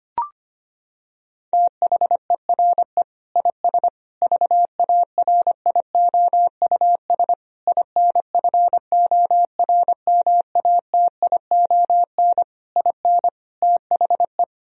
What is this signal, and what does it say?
Morse code, heard as a keyed tone, about 25 wpm, 700 Hz. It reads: THERE IS VARIOUS INFORMATION IN THE
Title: Sentences from Top 1000 Words - Review Part 4 - 25wpm